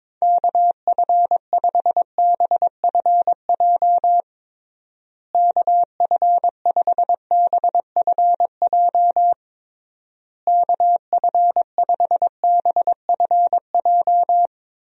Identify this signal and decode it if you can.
Morse code, heard as a keyed tone, about 23 wpm, 700 Hz. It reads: KF5BFJ KF5BFJ KF5BFJ
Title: USA Call Signs - 6 Characters - Repeated 3x - 22wpm